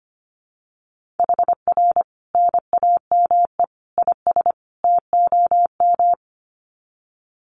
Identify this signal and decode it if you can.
Morse code, heard as a keyed tone, about 25 wpm, 700 Hz. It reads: HR NAME IS TOM
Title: Common QSO Phrases 25wpm